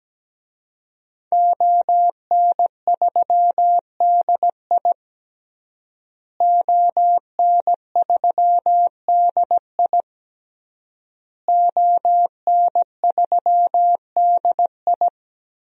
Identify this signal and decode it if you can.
Morse code, heard as a keyed tone, about 17 wpm, 700 Hz. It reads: ON3DI ON3DI ON3DI
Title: DXCC-100 Call Signs - 5 Characters - Repeated 3x - 17wpm